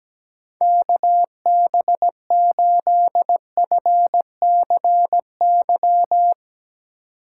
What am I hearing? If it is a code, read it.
Morse code, heard as a keyed tone, about 17 wpm, 700 Hz. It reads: KB8FCY
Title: USA Call Signs - 2x Character Spacing - 17wpm